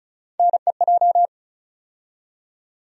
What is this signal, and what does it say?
Morse code, heard as a keyed tone, about 34 wpm, 700 Hz. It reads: NEJ
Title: USA Call Sign Postfixes - 3 Letters 35wpm